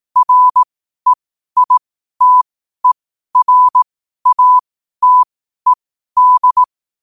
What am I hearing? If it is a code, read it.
Morse code, heard as a keyed tone, about 18 wpm, spaced as for 12 wpm, 1000 Hz. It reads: REITERATED
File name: qotdfiles / 12wpm